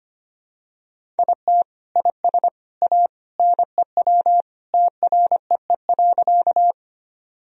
Morse code, heard as a keyed tone, about 25 wpm, 700 Hz. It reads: IT IS A NEW TREE.
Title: Sentences from Top 1000 Words - Review Part 1 - 25wpm